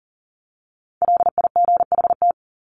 Morse code, heard as a keyed tone, about 40 wpm, 700 Hz. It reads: LIGHT
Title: Top 300 Words Repeated 40wpm